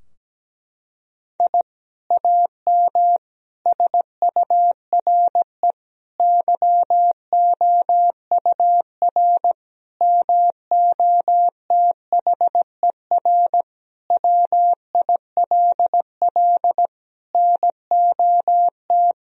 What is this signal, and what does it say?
Morse code, heard as a keyed tone, about 17 wpm, 700 Hz. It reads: I AM SURE YOUR MOTHER WILL NOT
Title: Sentences from Top 1000 Words - Review Part 3 - 17wpm